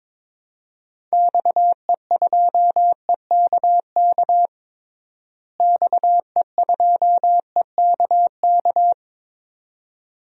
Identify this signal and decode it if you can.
Morse code, heard as a keyed tone, about 22 wpm, 700 Hz. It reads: XE2EKK XE2EKK
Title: DXCC-100 Call Signs - 6 Characters - Repeated 2x - 22wpm